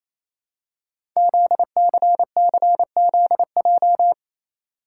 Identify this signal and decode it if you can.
Morse code, heard as a keyed tone, about 27 wpm, 690 Hz. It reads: ZCCZJ
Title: Code Groups - 5 Letters - 28wpm